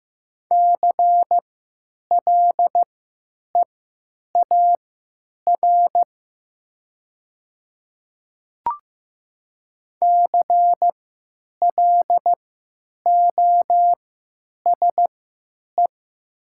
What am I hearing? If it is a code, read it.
Morse code, heard as a keyed tone, about 15 wpm, spaced as for 9 wpm, 700 Hz. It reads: CLEAR CLOSE
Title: 3x Character Spacing - Top 500 Words - 15wpm